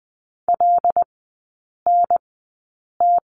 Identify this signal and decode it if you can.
Morse code, heard as a keyed tone, about 20 wpm, spaced as for 8 wpm, 700 Hz. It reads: LNT
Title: 074 Lesson 13e - Letter L - Three Characters 20wpm